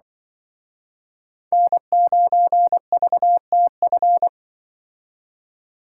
Morse code, heard as a keyed tone, about 24 wpm, 700 Hz. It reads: N9VTF
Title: Speed Racing - USA 1x3 Call Signs 20wpm